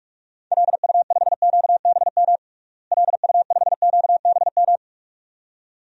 Morse code, heard as a keyed tone, about 44 wpm, 700 Hz. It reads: LU5QBM LU5QBM
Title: DXCC-100 Call Signs - 6 Characters - Repeated 2x - 45wpm